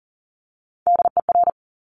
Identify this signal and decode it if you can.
Morse code, heard as a keyed tone, about 38 wpm, 700 Hz. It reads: DER
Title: Top Three Letter Combinations in Words 40wpm